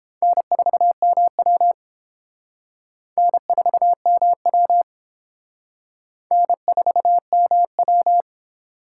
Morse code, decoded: N4MW N4MW N4MW

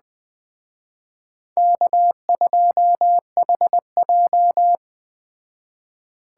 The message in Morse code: K2HJ

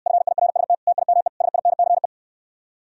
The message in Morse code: FIRST TIME SINCE

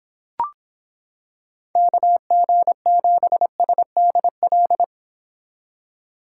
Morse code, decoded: KG7SDL

